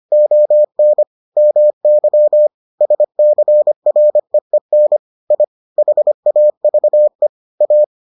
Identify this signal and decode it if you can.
Morse code, heard as a keyed tone, about 25 wpm, 600 Hz. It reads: ON MY SCREEN I HAVE A